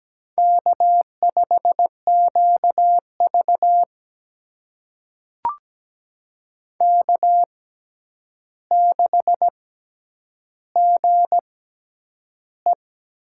Morse code, heard as a keyed tone, about 17 wpm, spaced as for 9 wpm, 700 Hz. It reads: K5QV K6GE
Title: USA Call Signs - 4 Characters - 6x Character Spacing - 17wpm